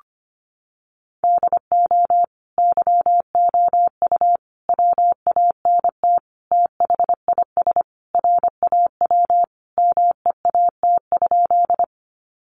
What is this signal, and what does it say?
Morse code, heard as a keyed tone, about 25 wpm, 700 Hz. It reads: DO YOU WANT THIS RAW MEAT?
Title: Sentences from Top 1500 Words 25wpm